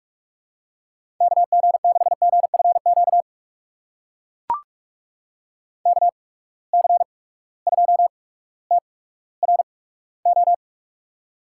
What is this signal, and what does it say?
Morse code, heard as a keyed tone, about 44 wpm, spaced as for 19 wpm, 700 Hz. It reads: KG6GFQ KC2TRO
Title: USA Call Signs - 6 Characters - 8x Character Spacing - 45wpm